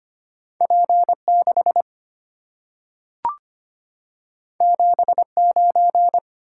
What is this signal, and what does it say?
Morse code, heard as a keyed tone, about 25 wpm, 700 Hz. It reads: P6 79
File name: Two Letter-Number Combinations - 25wpm